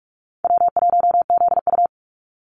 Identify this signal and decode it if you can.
Morse code, heard as a keyed tone, about 44 wpm, 700 Hz. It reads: W1ZU
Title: NA SOTA Activator Callsigns 45wpm